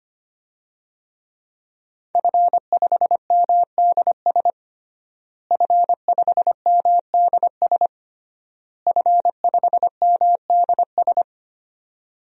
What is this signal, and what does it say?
Morse code, heard as a keyed tone, about 25 wpm, 700 Hz. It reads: F5MDS F5MDS F5MDS
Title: DXCC-100 Call Signs - 5 Characters - Repeated 3x - 25wpm